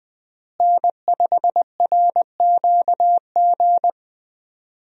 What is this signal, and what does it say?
Morse code, heard as a keyed tone, about 20 wpm, 700 Hz. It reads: N5RQG
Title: USA Call Signs - 5 Characters - Repeated 3x - 20wpm